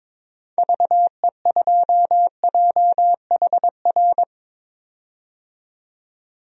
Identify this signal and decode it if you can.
Morse code, heard as a keyed tone, about 22 wpm, 700 Hz. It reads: VE2JHR
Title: DXCC-100 Call Signs - 6 Characters - 22wpm